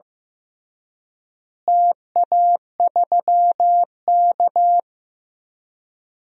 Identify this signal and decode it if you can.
Morse code, heard as a keyed tone, about 15 wpm, 700 Hz. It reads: TA3K
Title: DXCC-100 Call Signs - 4 Characters - Repeated 3x - 15wpm